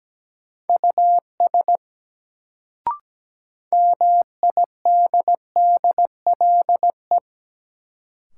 Morse code, heard as a keyed tone, about 17 wpm, 700 Hz. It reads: US MIDDLE
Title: Sets of 2 Words - New Word Drill - 17wpm